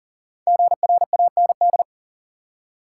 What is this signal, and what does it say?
Morse code, heard as a keyed tone, about 40 wpm, 700 Hz. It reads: GRAND
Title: Five Letter Words - 40wpm